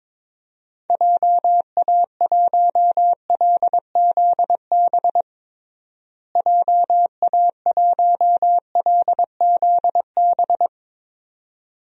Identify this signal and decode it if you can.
Morse code, heard as a keyed tone, about 22 wpm, 700 Hz. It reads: JA1LZB JA1LZB
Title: DXCC-100 Call Signs - 6 Characters - Repeated 2x - 22wpm